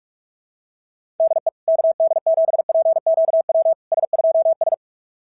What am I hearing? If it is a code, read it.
Morse code, heard as a keyed tone, about 44 wpm, 660 Hz. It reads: DE KD7PQW S2S
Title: SOTA QSO Practice 45wpm